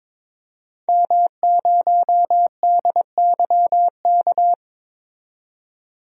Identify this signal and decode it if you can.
Morse code, heard as a keyed tone, about 22 wpm, 700 Hz. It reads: M0DYK
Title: DXCC-100 Call Signs - 5 Characters - 22wpm